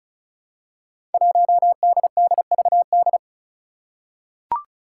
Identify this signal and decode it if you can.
Morse code, heard as a keyed tone, about 36 wpm, 700 Hz. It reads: 1DDVD
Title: Five Challenging Letter-Number Combinations 35wpm